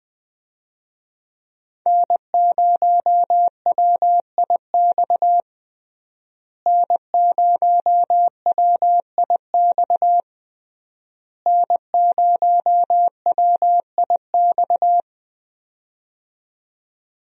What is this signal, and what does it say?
Morse code, heard as a keyed tone, about 20 wpm, 700 Hz. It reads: N0WIX N0WIX N0WIX
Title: USA Call Signs - 5 Characters - Repeated 3x - 20wpm